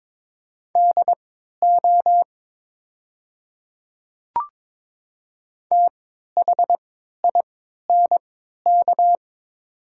Morse code, heard as a keyed tone, about 22 wpm, spaced as for 12 wpm, 690 Hz. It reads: DO THINK
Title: 3x Character Spacing - Top 200 Words - 22wpm